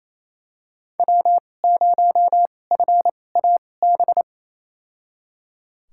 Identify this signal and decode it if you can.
Morse code, heard as a keyed tone, about 28 wpm, spaced as for 20 wpm, 700 Hz. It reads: W0FAB
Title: USA Call Signs - 5 Characters - 2x Character Spacing - 28wpm